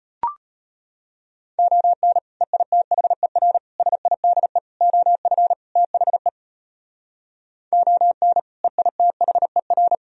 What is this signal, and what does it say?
Morse code, heard as a keyed tone, about 31 wpm, 700 Hz. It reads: ON EITHER SIDE OF THE ON EITHER